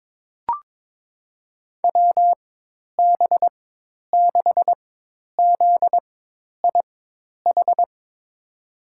Morse code, heard as a keyed tone, about 21 wpm, spaced as for 10 wpm, 700 Hz. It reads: WB6ZIH
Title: USA Call Signs - 6 Characters - 4x Character Spacing - 22wpm